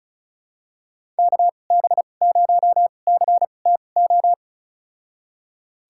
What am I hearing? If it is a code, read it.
Morse code, heard as a keyed tone, about 36 wpm, spaced as for 25 wpm, 700 Hz. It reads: KB0CTO
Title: USA Call Signs - 6 Characters - 2x Character Spacing - 35wpm